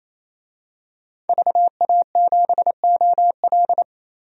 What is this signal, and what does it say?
Morse code, heard as a keyed tone, about 28 wpm, 700 Hz. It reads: VA7OL
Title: DXCC-100 Call Signs - 5 Characters - 28wpm